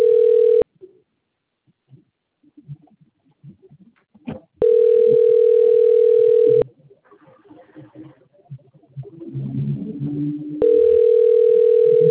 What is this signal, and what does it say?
Ringback tone (North America), 440 + 480 Hz, with speech.
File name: rg-1600-unknown- - - .50097